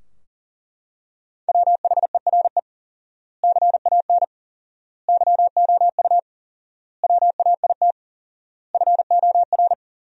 Morse code, heard as a keyed tone, about 40 wpm, 700 Hz. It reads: WHERE CAN YOU WAIT FOR
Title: 4x Word-Spacing - Sentences from Top 500 Words - 40wpm